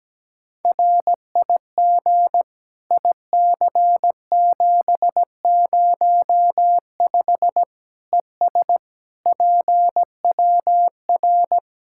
Morse code, heard as a keyed tone, about 17 wpm, 700 Hz. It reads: RIG IC705 ES PWR
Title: Practice QSO - Exchange 2 - Info - 17wpm